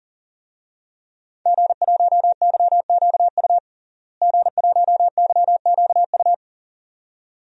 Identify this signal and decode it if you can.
Morse code, heard as a keyed tone, about 39 wpm, 700 Hz. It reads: G1YQU G1YQU
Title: DXCC-100 Call Signs - 5 Characters - Repeated 2x - 40wpm